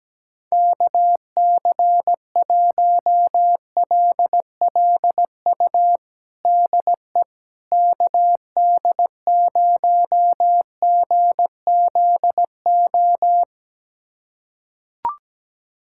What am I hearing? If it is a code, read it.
Morse code, heard as a keyed tone, about 17 wpm, 700 Hz. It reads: KC1LLU DE KD0GZO
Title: USA Callsign DE Callsign 17wpm